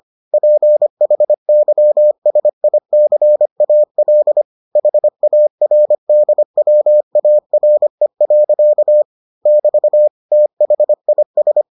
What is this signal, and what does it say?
Morse code, heard as a keyed tone, about 25 wpm, 600 Hz. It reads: PHYSICAL HARDWARE. <BT> THIS